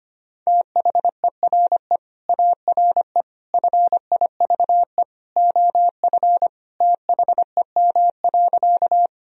Morse code, heard as a keyed tone, about 25 wpm, 700 Hz. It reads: THERE ARE FIVE OF THEM.
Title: Sentences from Top 1000 Words - Review Part 2 - 25wpm